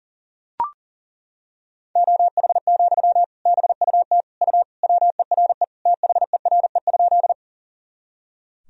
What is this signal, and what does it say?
Morse code, heard as a keyed tone, about 40 wpm, 700 Hz. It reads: OH, BUT U WERE THERE?
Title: Sentences from Top 500 Words and Abbreviations 40wpm